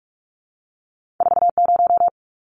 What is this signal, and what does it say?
Morse code, heard as a keyed tone, about 46 wpm, 700 Hz. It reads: 40
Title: Numbers - 2-Digit 45wpm